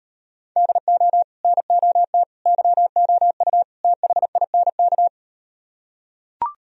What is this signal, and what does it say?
Morse code, heard as a keyed tone, about 38 wpm, 700 Hz. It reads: DO NOT YOU THINK